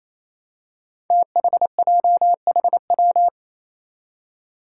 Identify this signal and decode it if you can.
Morse code, heard as a keyed tone, about 28 wpm, 700 Hz. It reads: THJHW
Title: Five Challenging Letter-Number Combinations 28wpm